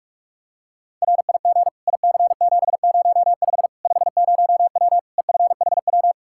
Morse code, heard as a keyed tone, about 41 wpm, 700 Hz. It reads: RIG IC705 50W EFHW
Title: Practice QSO - Exchange 2 - Info - 45wpm